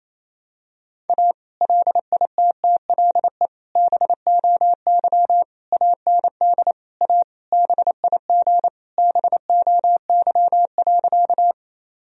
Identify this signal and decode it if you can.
Morse code, heard as a keyed tone, about 28 wpm, 700 Hz. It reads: A LITTLE BOY AND A BIG BOY.
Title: Sentences from Top 1000 Words - Review Part 1 - 28wpm